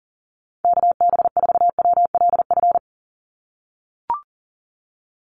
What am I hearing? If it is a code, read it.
Morse code, heard as a keyed tone, about 40 wpm, 700 Hz. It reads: KB4WLF
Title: Call Signs - 40wpm